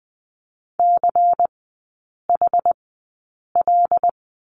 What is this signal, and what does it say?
Morse code, heard as a keyed tone, about 20 wpm, spaced as for 8 wpm, 700 Hz. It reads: CHL
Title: 133 Lesson 22e - Question Mark - Three Characters 20wpm